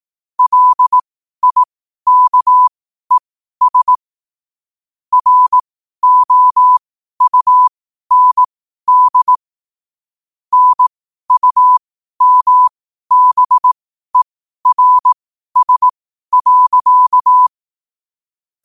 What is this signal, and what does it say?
Morse code, heard as a keyed tone, about 18 wpm, spaced as for 12 wpm, 1000 Hz. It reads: LIKES ROUND NUMBERS.
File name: qotdfiles / 12wpm